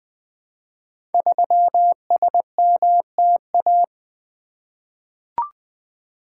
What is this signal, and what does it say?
Morse code, heard as a keyed tone, about 20 wpm, 700 Hz. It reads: 3SMTA
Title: Code Groups - 5 Letter:Numbers - 20wpm